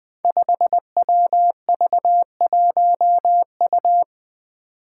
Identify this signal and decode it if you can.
Morse code, heard as a keyed tone, about 20 wpm, 700 Hz. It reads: 5WV1U
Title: Five Challenging Letter-Number Combinations 20wpm